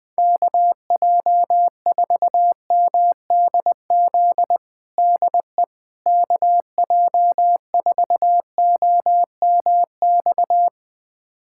Morse code, heard as a keyed tone, about 21 wpm, 700 Hz. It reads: KJ4MDZ DE KJ4OMX
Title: USA Callsign DE Callsign - 6 Characters - 20wpm